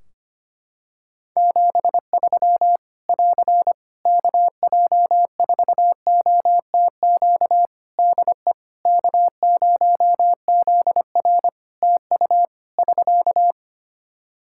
Morse code, heard as a keyed tone, about 25 wpm, 700 Hz. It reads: 73 <AR> KJ4OTQ DE K0ZR TU <SK>
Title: Practice QSO - Exchange 4 - Ending - 25wpm